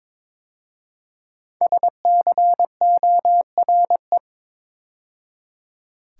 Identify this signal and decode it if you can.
Morse code, heard as a keyed tone, about 22 wpm, 700 Hz. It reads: SCORE
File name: 5 Letter Words - First 3 Letters Given - 22wpm